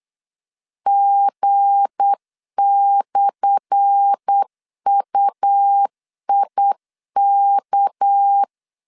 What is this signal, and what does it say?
Morse code, heard as a keyed tone, about 8 wpm, 800 Hz. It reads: G/UIK
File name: LEZIONI / 10 SIGNS / 10A SLASH 8wpm 3452